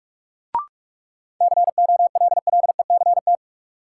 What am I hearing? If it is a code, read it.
Morse code, heard as a keyed tone, about 46 wpm, 700 Hz. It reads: COLLECT
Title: Sentences from Top 1600 Words - New Word Drill - 45wpm